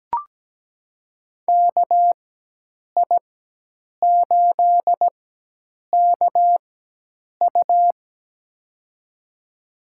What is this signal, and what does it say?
Morse code, heard as a keyed tone, about 17 wpm, spaced as for 8 wpm, 700 Hz. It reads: KI8KU
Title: USA Call Signs - 5 Characters - 4x Character Spacing - 17wpm